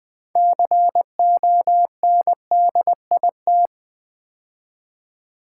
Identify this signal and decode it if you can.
Morse code, heard as a keyed tone, about 20 wpm, 700 Hz. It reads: CONDIT
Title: Build A Word - 3 Letters At A Time - 9-Letter Words - 20wpm